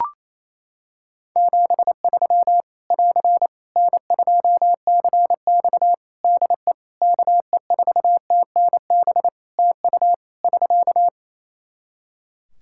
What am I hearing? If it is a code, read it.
Morse code, heard as a keyed tone, about 28 wpm, 700 Hz. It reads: 73 <AR> N2CX DE KE4TNB TU <SK>
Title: Practice QSO - Exchange 4 - Ending - 28wpm